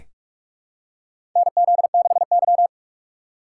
Morse code, heard as a keyed tone, about 42 wpm, 700 Hz. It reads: NZ6Y
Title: USA Call Signs - 4 Characters - 8x Character Spacing - 45wpm